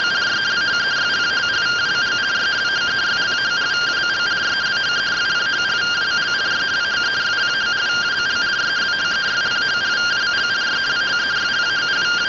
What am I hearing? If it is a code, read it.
Two-tone teleprinter signal, about 170 Hz shift.